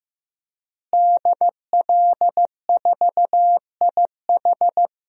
Morse code, heard as a keyed tone, about 15 wpm, 700 Hz. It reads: DL4IH
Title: DXCC-100 Call Signs - 5 Characters - Repeated 3x - 15wpm